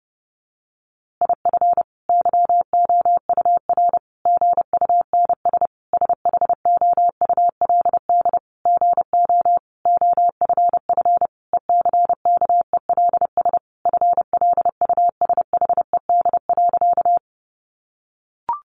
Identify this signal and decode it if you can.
Morse code, heard as a keyed tone, about 30 wpm, 700 Hz. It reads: IF YOUR GUNS SHOULD GO OFF ECKELS FLUSHED.
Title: A Sound Of Thunder 30wpm